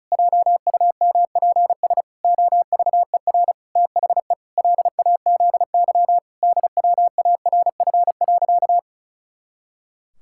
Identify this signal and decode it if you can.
Morse code, heard as a keyed tone, about 34 wpm, 700 Hz. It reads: JUMPS OVER THE LAZY DWARF.
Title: Pangram Challenge v2 - 35wpm